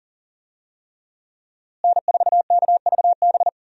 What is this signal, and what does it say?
Morse code, heard as a keyed tone, about 40 wpm, 700 Hz. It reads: N4KVB
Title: USA Call Signs - 5 Characters - Repeated 3x - 40wpm